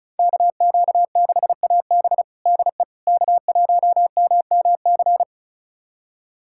Morse code, heard as a keyed tone, about 34 wpm, 700 Hz. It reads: KQ6AB DE K1MMC
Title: USA Callsign DE Callsign - 5 Characters - 35wpm